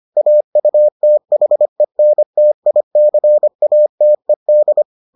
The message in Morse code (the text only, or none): AUTHENTICATED